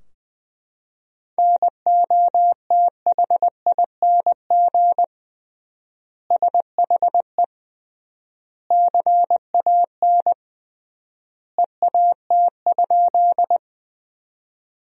Morse code, heard as a keyed tone, about 20 wpm, 700 Hz. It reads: NOTHING SHE CAN EAT?
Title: 3x Word-Spacing - Sentences from Top 500 Words - 20wpm